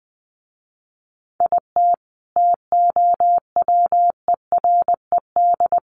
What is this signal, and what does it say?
Morse code, heard as a keyed tone, about 19 wpm, 700 Hz. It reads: IT TOWERED
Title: A Sound Of Thunder 20wpm